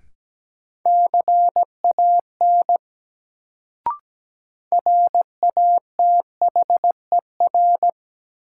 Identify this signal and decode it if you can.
Morse code, heard as a keyed tone, about 17 wpm, 700 Hz. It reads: CAN RATHER
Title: Sets of 2 Words - New Word Drill - 17wpm